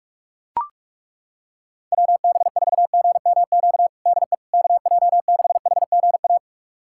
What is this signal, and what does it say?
Morse code, heard as a keyed tone, about 41 wpm, 700 Hz. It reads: WB4GMQ DE KJ6HGA
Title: USA Callsign DE Callsign - 6 Characters - 45wpm